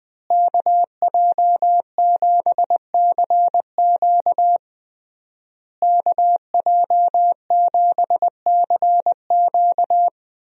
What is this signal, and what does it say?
Morse code, heard as a keyed tone, about 21 wpm, 700 Hz. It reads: KJ7CQ KJ7CQ
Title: USA Call Signs - 5 Characters - Repeated 3x - 20wpm